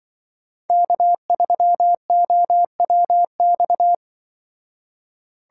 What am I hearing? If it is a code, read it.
Morse code, heard as a keyed tone, about 24 wpm, 700 Hz. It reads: K3OWX
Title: Speed Racing - USA 1x3 Call Signs 20wpm